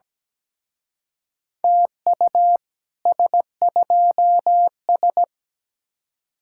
Morse code, heard as a keyed tone, about 20 wpm, 700 Hz. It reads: TU S2S
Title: Speed-Racing Common QSO Phrases Encore - 17wpm